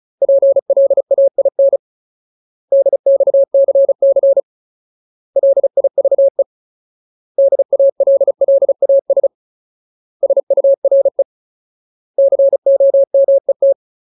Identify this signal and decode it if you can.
Morse code, heard as a keyed tone, about 34 wpm, 550 Hz. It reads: PLAIN DXCC LIVE DALLAS SURE COMET
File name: Ultimate Word List / 35WPM Ultimate